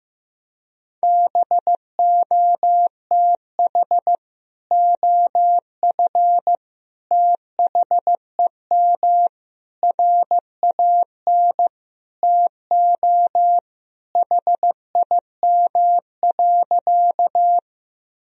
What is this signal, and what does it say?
Morse code, heard as a keyed tone, about 14 wpm, 700 Hz. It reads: BOTH OF THEM RAN TO HIM.
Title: Sentences from Top 1000 Words - Review Part 3 - 15wpm